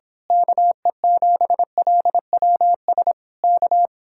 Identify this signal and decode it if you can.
Morse code, heard as a keyed tone, about 26 wpm, 700 Hz. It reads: KE7LWS K
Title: Speed-Racing Common QSO Phrases - 22wpm